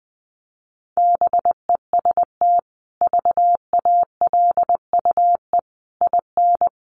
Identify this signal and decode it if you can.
Morse code, heard as a keyed tone, about 20 wpm, 700 Hz. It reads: BEST VALUE IN
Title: Sentences from Top 600 Words 20wpm